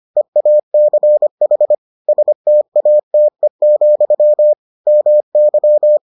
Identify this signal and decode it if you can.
Morse code, heard as a keyed tone, about 25 wpm, 600 Hz. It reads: EACH STATE, MY